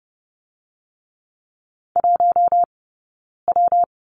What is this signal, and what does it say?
Morse code, heard as a keyed tone, about 30 wpm, 700 Hz. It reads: 1 W
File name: Lesson 19d - Letter w - Two Characters 30wpm